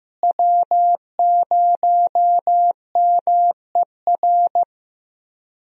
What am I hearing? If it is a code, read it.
Morse code, heard as a keyed tone, about 15 wpm, 700 Hz. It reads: W0MER
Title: USA Call Signs - 5 Characters - Repeated 3x - 15wpm